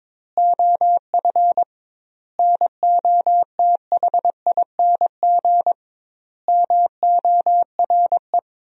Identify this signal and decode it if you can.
Morse code, heard as a keyed tone, about 22 wpm, 700 Hz. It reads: OF NOTHING MORE
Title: 2x Word-Spacing - Sentences from Top 500 Words - 22wpm